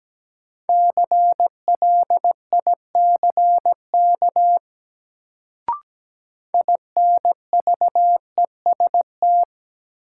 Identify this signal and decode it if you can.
Morse code, heard as a keyed tone, about 17 wpm, 700 Hz. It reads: CLICK INVEST
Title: Sentences from Top 1100 Words - New Word Drill Alt - 17wpm